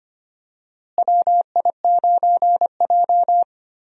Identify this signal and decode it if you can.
Morse code, heard as a keyed tone, about 25 wpm, 700 Hz. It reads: WI9J
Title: USA Call Signs - 4 Characters - Repeated 3x - 25wpm